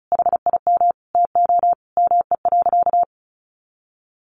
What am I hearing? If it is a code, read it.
Morse code, heard as a keyed tone, about 34 wpm, 700 Hz. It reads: HIM TO ME.
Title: Sentences from Top 1500 Words 35wpm